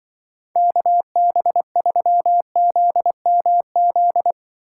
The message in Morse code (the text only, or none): KB3ZMZ